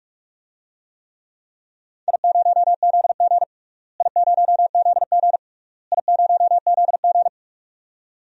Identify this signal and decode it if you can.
Morse code, heard as a keyed tone, about 41 wpm, 700 Hz. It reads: I0ZG I0ZG I0ZG
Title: DXCC-100 Call Signs - 4 Characters - Repeated 3x - 45wpm